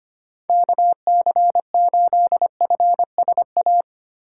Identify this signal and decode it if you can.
Morse code, heard as a keyed tone, about 25 wpm, 700 Hz. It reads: KC8FSA
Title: USA Call Signs - 8x Character Spacing - 25wpm